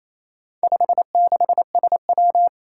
Morse code, heard as a keyed tone, about 27 wpm, 700 Hz. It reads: 56SW
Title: Four Challenging Letter-Number Combinations 28wpm